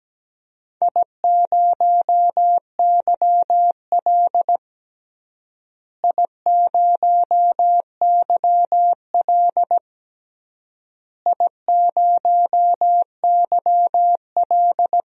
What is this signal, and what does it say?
Morse code, heard as a keyed tone, about 17 wpm, 700 Hz. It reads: I0YL I0YL I0YL
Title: DXCC-100 Call Signs - 4 Characters - Repeated 3x - 17wpm